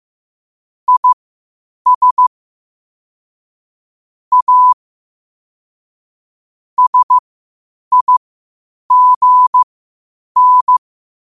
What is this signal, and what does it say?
Morse code, heard as a keyed tone, about 15 wpm, spaced as for 8 wpm, 1000 Hz. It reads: IS A SIGN
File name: qotdfiles / 8wpm